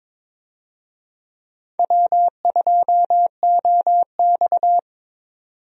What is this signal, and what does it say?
Morse code, heard as a keyed tone, about 21 wpm, 700 Hz. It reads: W2OX
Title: USA Call Signs - 4 Characters - 22wpm